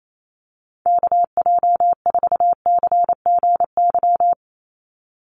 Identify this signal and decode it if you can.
Morse code, heard as a keyed tone, about 28 wpm, 700 Hz. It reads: KJ4CGY
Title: USA Call Signs 28wpm